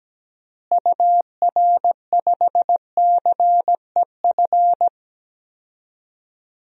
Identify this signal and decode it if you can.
Morse code, heard as a keyed tone, about 17 wpm, 700 Hz. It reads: UR5CEF
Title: DXCC-100 Call Signs - 6 Characters - Repeated 3x - 17wpm